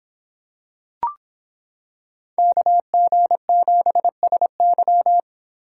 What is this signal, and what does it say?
Morse code, heard as a keyed tone, about 26 wpm, 700 Hz. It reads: KG7SY